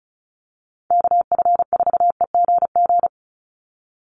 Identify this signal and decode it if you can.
Morse code, heard as a keyed tone, about 34 wpm, 700 Hz. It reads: KF4EGG
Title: USA Call Signs 35wpm